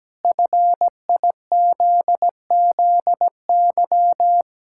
Morse code, heard as a keyed tone, about 17 wpm, 700 Hz. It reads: FIZZY
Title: Pangram Challenge v2 - 17wpm